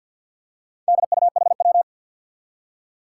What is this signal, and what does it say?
Morse code, heard as a keyed tone, about 47 wpm, 700 Hz. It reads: DUHW
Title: Four Challenging Letter-Number Combinations 50wpm